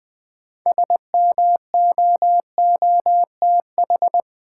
Morse code, heard as a keyed tone, about 21 wpm, 700 Hz. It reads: SMOOTH
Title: Sentences from Top 1400 Words - New Word Drill Alt - 20wpm